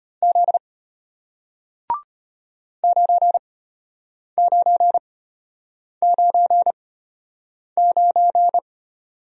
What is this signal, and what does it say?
Morse code, heard as a keyed tone, about 32 wpm, spaced as for 8 wpm, 700 Hz. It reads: Z 9999